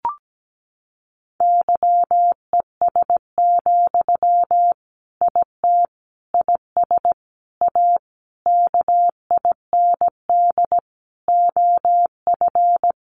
Morse code, heard as a keyed tone, about 17 wpm, 700 Hz. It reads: YES, IT IS A KIND OF
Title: Sentences from Top 1400 Words 17wpm